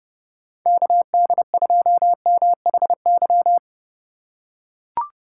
Morse code, heard as a keyed tone, about 30 wpm, 700 Hz. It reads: KD2MHY